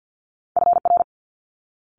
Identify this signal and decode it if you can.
Morse code, heard as a keyed tone, about 50 wpm, 700 Hz. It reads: FR
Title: Multi-letter Phonemes Quick Fire 50wpm